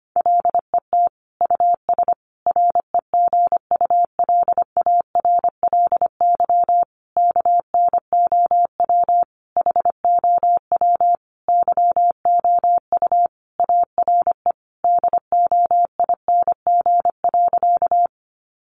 Morse code, heard as a keyed tone, about 25 wpm, 700 Hz. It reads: LET US REGULARLY KNOW HOW YOU ARE DOING.
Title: Sentences from Top 1400 Words 25wpm